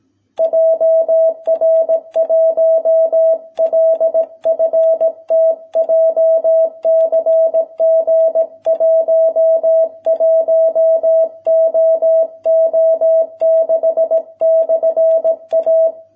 Morse code, heard as a keyed tone, about 18 wpm, 650 Hz. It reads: JR1LFT JCG 11OO6/A